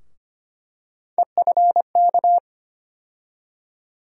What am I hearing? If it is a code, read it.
Morse code, heard as a keyed tone, about 25 wpm, 700 Hz. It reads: EFK
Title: USA Call Sign Postfixes - 3 Letters 25wpm